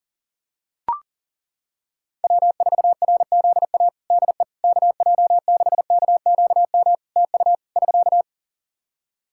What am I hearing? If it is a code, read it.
Morse code, heard as a keyed tone, about 40 wpm, 700 Hz. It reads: W4RZA DE KJ6KQM TU <SK>
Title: Practice QSO - Exchange 4 - Ending - 40wpm